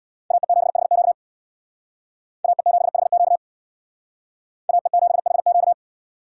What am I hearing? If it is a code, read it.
Morse code, heard as a keyed tone, about 62 wpm, 700 Hz. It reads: AE7HQ AE7HQ AE7HQ
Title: Speed Racing - USA 2x2 Call Signs 50wpm